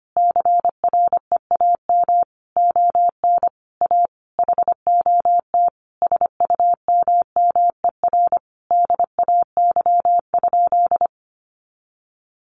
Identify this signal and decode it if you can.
Morse code, heard as a keyed tone, about 25 wpm, 700 Hz. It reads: CREAM ON A HOT SUMMER DAY?
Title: Sentences from Top 1500 Words 25wpm